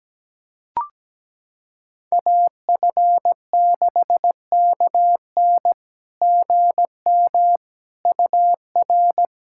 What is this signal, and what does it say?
Morse code, heard as a keyed tone, about 17 wpm, 700 Hz. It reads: AF6KN GM UR
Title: SOTA QSO Practice 17wpm